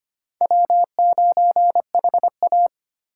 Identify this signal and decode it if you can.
Morse code, heard as a keyed tone, about 32 wpm, 700 Hz. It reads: W9HA